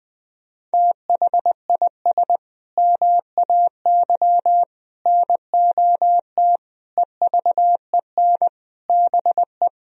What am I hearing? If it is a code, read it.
Morse code, heard as a keyed tone, about 20 wpm, 700 Hz. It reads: THIS MAY NOT EVEN BE
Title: Sentences from Top 1000 Words - Review Part 3 - 20wpm